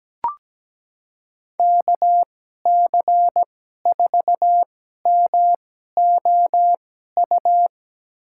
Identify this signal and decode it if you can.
Morse code, heard as a keyed tone, about 17 wpm, spaced as for 12 wpm, 700 Hz. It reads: KC4MOU
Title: USA Call Signs - 6 Characters - 2x Character Spacing - 17wpm